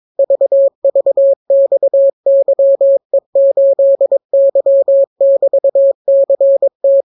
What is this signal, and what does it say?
Morse code, heard as a keyed tone, about 22 wpm, 550 Hz. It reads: VVXYE8Y<BT>CT